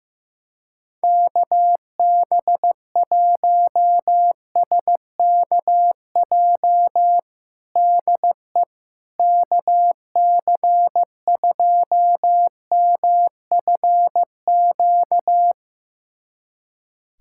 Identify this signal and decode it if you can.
Morse code, heard as a keyed tone, about 15 wpm, 700 Hz. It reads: KB1SKJ DE KC2MFQ
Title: USA Callsign DE Callsign 15wpm